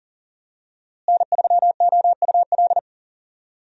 Morse code, heard as a keyed tone, about 40 wpm, 690 Hz. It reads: N3OUL
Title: USA Call Signs - 5 Characters - Repeated 3x - 40wpm